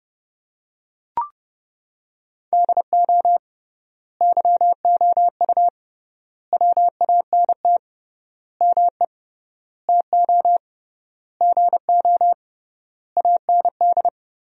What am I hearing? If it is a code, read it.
Morse code, heard as a keyed tone, about 30 wpm, 700 Hz. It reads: DO YOU WANT ME TO GO AND
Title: 3x Word-Spacing - Sentences from Top 200 Words - 30wpm